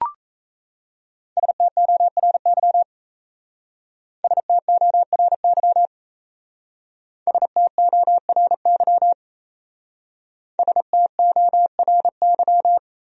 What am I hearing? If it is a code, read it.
Morse code, heard as a keyed tone, about 34 wpm, 690 Hz. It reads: STORY STORY STORY STORY
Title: Speed-Racing Top 500 Words 28wpm